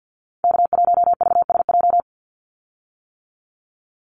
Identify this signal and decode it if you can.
Morse code, heard as a keyed tone, about 48 wpm, 700 Hz. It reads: K1VSJ
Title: Call Signs - 50wpm